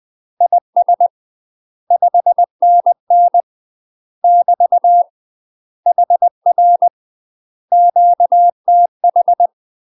Morse code, heard as a keyed tone, about 20 wpm, 700 Hz. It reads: IS 5NN <BT> HR QTH
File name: UK MASTER QSOs TRAINING/20WPM